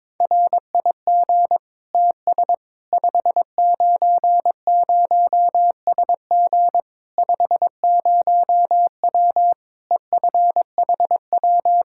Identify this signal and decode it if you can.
Morse code, heard as a keyed tone, about 22 wpm, 700 Hz. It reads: RIG TS 590SG 50W EFHW
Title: Practice QSO - Exchange 2 - Info - 22wpm